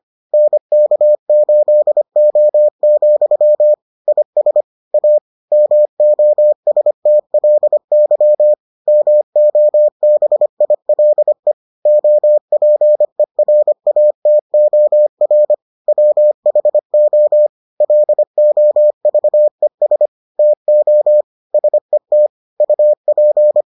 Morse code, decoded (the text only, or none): NK8O, IS A MOSTLY MOBILE OPERATOR WHO LOVES TO SET UP